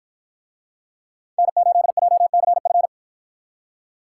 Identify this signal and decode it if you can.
Morse code, heard as a keyed tone, about 53 wpm, 700 Hz. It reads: N8JCF